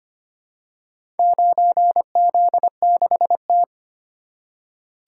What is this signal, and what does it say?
Morse code, heard as a keyed tone, about 25 wpm, 700 Hz. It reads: 9Z6T
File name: practice / 25wpm / Four Letter-Number Combinations - 25wpm